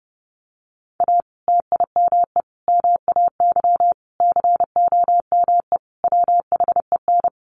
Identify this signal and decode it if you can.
Morse code, heard as a keyed tone, about 30 wpm, 700 Hz. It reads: A TIME MAY COME WHEN
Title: Sentences from Top 1500 Words 30wpm